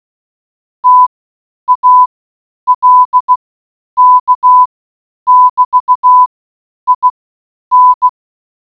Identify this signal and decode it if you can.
Morse code, heard as a keyed tone, about 16 wpm, spaced as for 9 wpm, 1000 Hz. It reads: TALK<BT>IN